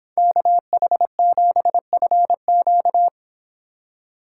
Morse code, heard as a keyed tone, about 26 wpm, 700 Hz. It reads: KH7FQ